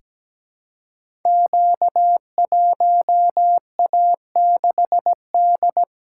Morse code, heard as a keyed tone, about 17 wpm, 700 Hz. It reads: Q1A6D
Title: Code Groups - 5 Letter:Numbers - 17wpm